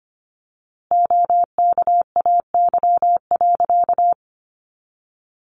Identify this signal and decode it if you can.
Morse code, heard as a keyed tone, about 25 wpm, 700 Hz. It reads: OKAY.
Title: A Sound Of Thunder 25wpm